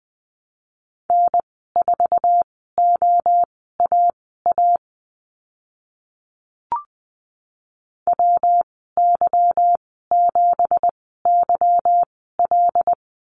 Morse code, heard as a keyed tone, about 20 wpm, spaced as for 14 wpm, 700 Hz. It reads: N4OAA WY7YL
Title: 173 Lesson 28g - Number 9 - Call Signs 20wpm